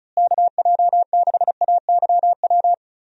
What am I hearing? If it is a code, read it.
Morse code, heard as a keyed tone, about 34 wpm, 700 Hz. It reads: KJ6AYW
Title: USA Call Signs - 6 Characters - Repeated 3x - 35wpm